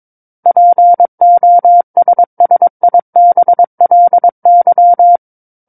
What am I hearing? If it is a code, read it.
Morse code, heard as a keyed tone, about 22 wpm, 700 Hz. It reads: POSSIBLY
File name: 06-Chapter-06 22 WPM